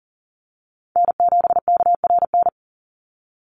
Morse code, heard as a keyed tone, about 40 wpm, 700 Hz. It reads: N7KRN
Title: NA SOTA Chaser Callsigns 40wpm